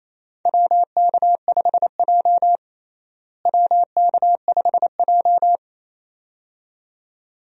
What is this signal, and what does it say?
Morse code, heard as a keyed tone, about 28 wpm, 700 Hz. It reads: WK5J WK5J
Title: USA Call Signs - 4 Characters - Repeated 2x - 28wpm